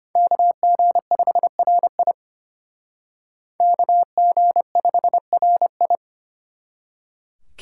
Morse code, heard as a keyed tone, about 32 wpm, 700 Hz. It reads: KG5RI KG5RI